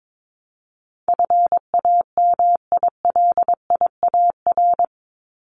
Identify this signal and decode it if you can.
Morse code, heard as a keyed tone, about 22 wpm, 700 Hz. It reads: FAMILIAR
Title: Top 751-1000 Words 22wpm